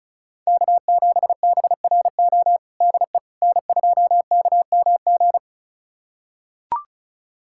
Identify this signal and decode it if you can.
Morse code, heard as a keyed tone, about 36 wpm, 690 Hz. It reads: K7BRO DE N2KMG
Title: USA Callsign DE Callsign - 5 Characters - 35wpm